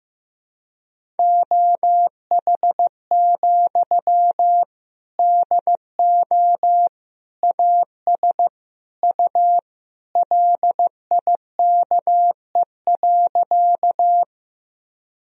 Morse code, heard as a keyed tone, about 15 wpm, 700 Hz. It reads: OH, DO AS U LIKE.
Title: Sentences from Top 500 Words and Abbreviations 15wpm